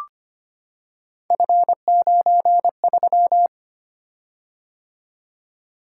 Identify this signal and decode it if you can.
Morse code, heard as a keyed tone, about 25 wpm, 700 Hz. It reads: F93
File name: Three Letter-Number Combinations - 25wpm